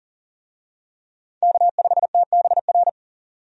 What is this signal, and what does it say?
Morse code, heard as a keyed tone, about 40 wpm, 700 Hz. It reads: K5TBR
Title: USA Call Signs - 5 Characters - 40wpm